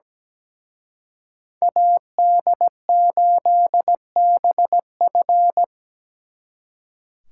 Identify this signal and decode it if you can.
Morse code, heard as a keyed tone, about 20 wpm, 700 Hz. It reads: AD8BF
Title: Speed Racing - USA 2x2 Call Signs 17wpm